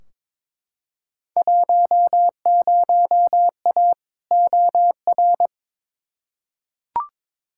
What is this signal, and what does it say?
Morse code, heard as a keyed tone, about 22 wpm, 700 Hz. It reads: 10A OR
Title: Field Day Exchanges - 22wpm